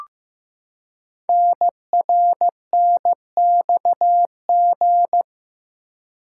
Morse code, heard as a keyed tone, about 15 wpm, 700 Hz. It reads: NRNXG
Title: Code Groups - 5 Letters - 15wpm